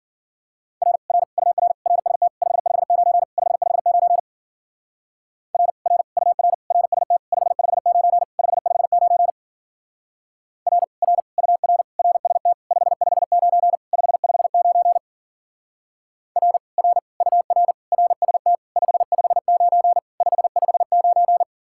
Morse code, heard as a keyed tone, about 49 wpm, 700 Hz. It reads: R R UR RST 559 559 R R UR RST 559 559 R R UR RST 559 559 R R UR RST 559 559
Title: Speed-Racing Common QSO Phrases - 40wpm